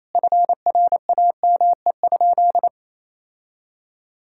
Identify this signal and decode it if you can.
Morse code, heard as a keyed tone, about 28 wpm, 700 Hz. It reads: FRAME?
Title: Sentences from Top 1000 Words - Review Part 6 - 28wpm